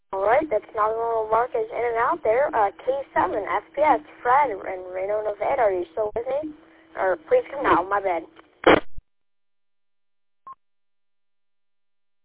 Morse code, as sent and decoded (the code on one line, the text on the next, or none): none